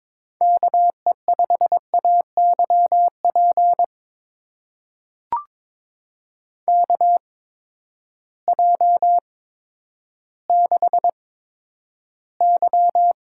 Morse code, heard as a keyed tone, about 22 wpm, spaced as for 11 wpm, 700 Hz. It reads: KE5AYP KJ6Y
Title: USA Call Signs - 6 Characters - 8x Character Spacing - 22wpm